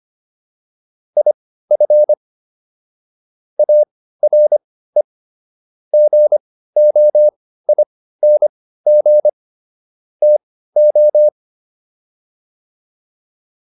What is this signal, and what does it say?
Morse code, heard as a keyed tone, about 25 wpm, spaced as for 15 wpm, 600 Hz. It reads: IF ARE GOING TO
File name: WD404 15